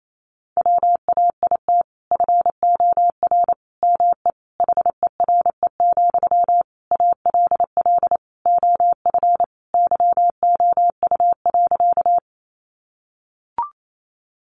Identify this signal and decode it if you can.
Morse code, heard as a keyed tone, about 28 wpm, 700 Hz. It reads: WAIT FOR ME HERE, ALL OF YOU.
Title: Sentences from Top 500 Words 28wpm